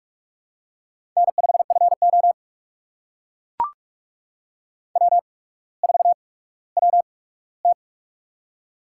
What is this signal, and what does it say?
Morse code, heard as a keyed tone, about 44 wpm, spaced as for 19 wpm, 700 Hz. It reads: N5FO W4WT
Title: USA Call Signs - 4 Characters - 8x Character Spacing - 45wpm